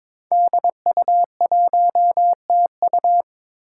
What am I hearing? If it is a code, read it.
Morse code, heard as a keyed tone, about 22 wpm, 700 Hz. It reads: DU1TU
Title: Five Challenging Letter-Number Combinations 22wpm